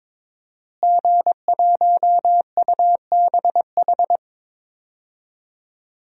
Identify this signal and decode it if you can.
Morse code, heard as a keyed tone, about 22 wpm, 700 Hz. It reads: G1UBH
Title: DXCC-100 Call Signs - 5 Characters - Repeated 3x - 22wpm